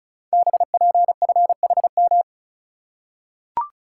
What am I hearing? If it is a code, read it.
Morse code, heard as a keyed tone, about 36 wpm, 700 Hz. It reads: BPFHM
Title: Code Groups - 5 Letters - 35wpm